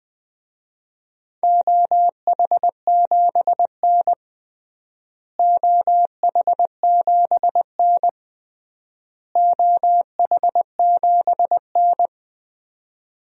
Morse code, heard as a keyed tone, about 21 wpm, 700 Hz. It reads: OH7N OH7N OH7N
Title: DXCC-100 Call Signs - 4 Characters - Repeated 3x - 20wpm